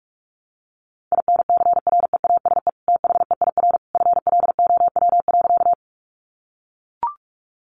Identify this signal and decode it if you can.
Morse code, heard as a keyed tone, about 44 wpm, 700 Hz. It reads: INCREASE THEIR FLOW.
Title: Sentences from Top 1200 Words 45wpm